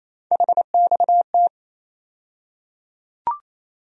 Morse code, heard as a keyed tone, about 28 wpm, 700 Hz. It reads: HXT
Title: USA Call Sign Postfixes - 3 Letters 28wpm